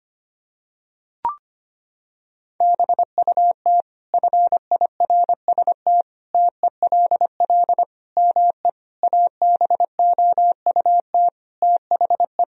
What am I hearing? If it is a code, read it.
Morse code, heard as a keyed tone, about 25 wpm, 700 Hz. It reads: BUT FIRST TELL ME ABOUT THE
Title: Sentences from Top 1000 Words - Review Part 2 - 25wpm